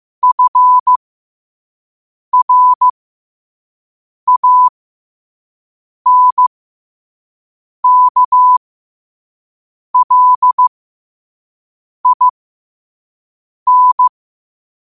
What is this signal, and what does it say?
Morse code, heard as a keyed tone, about 15 wpm, spaced as for 5 wpm, 1000 Hz. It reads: FRANKLIN
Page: New quotes every day in morse code at 5 Words per minute.